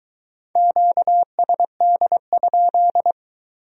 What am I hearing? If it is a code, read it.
Morse code, heard as a keyed tone, about 22 wpm, 700 Hz. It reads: QSD?